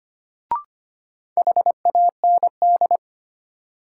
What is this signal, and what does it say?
Morse code, heard as a keyed tone, about 25 wpm, 700 Hz. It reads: HAND
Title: Sentences from Top 200 Words - New Word Drill - 25wpm